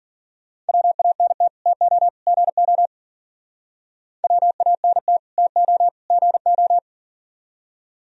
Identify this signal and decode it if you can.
Morse code, heard as a keyed tone, about 52 wpm, 700 Hz. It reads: WANT TO GO WANT TO GO